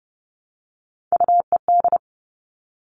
Morse code, heard as a keyed tone, about 30 wpm, 700 Hz. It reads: UED
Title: All Cut Numbers - 3-Digit 30wpm